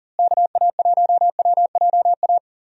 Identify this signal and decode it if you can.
Morse code, heard as a keyed tone, about 40 wpm, 700 Hz. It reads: KA1WJA
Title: USA Call Signs - 6 Characters - Repeated 3x - 40wpm